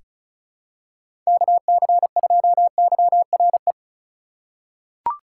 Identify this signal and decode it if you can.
Morse code, heard as a keyed tone, about 36 wpm, 700 Hz. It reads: KC2YRE
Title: USA Call Signs - 6 Characters - 2x Character Spacing - 35wpm